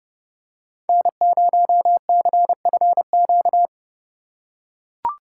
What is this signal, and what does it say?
Morse code, heard as a keyed tone, about 29 wpm, 700 Hz. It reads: N0CFQ